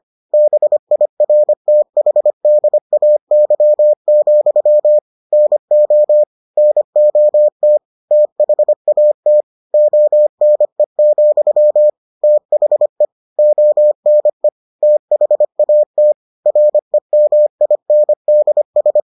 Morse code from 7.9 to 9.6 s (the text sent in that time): THAT